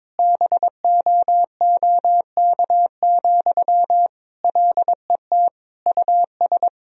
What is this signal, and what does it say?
Morse code, heard as a keyed tone, about 22 wpm, 700 Hz. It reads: BOOK, LET US
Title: Sentences from Top 1000 Words - Review Part 4 - 22wpm